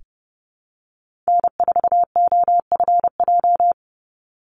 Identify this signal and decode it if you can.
Morse code, heard as a keyed tone, about 30 wpm, 700 Hz. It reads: N4OFJ
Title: USA Call Signs 30wpm